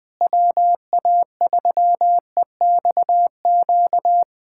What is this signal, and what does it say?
Morse code, heard as a keyed tone, about 21 wpm, 700 Hz. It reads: WA3EXQ
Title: USA Callsign DE Callsign - 6 Characters - 20wpm